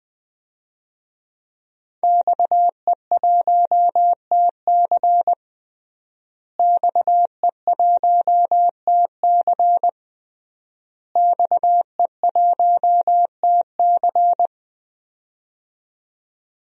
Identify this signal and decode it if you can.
Morse code, heard as a keyed tone, about 20 wpm, 700 Hz. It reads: XE1TC XE1TC XE1TC
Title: DXCC-100 Call Signs - 5 Characters - Repeated 3x - 20wpm